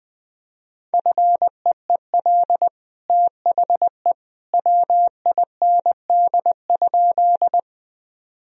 Morse code, heard as a keyed tone, about 20 wpm, 700 Hz. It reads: FEEL THE WIND?
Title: Sentences from Top 1000 Words - Review Part 2 - 20wpm